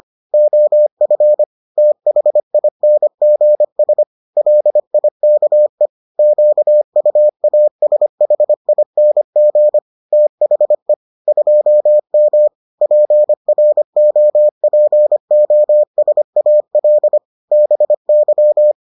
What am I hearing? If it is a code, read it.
Morse code, heard as a keyed tone, about 25 wpm, 600 Hz. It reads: OF THINGS LIKE QUASHING THE 2M PROPOSAL BY